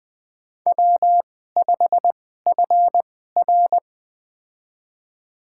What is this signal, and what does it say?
Morse code, heard as a keyed tone, about 20 wpm, spaced as for 15 wpm, 700 Hz. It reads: W5FR
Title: USA Call Signs - 4 Characters - 2x Character Spacing - 20wpm